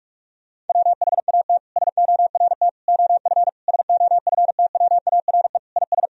Morse code, heard as a keyed tone, about 41 wpm, 700 Hz. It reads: WHAT SORT OF SOFTWARE IS
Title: Sentences from Top 1000 Words - Review Part 3 - 45wpm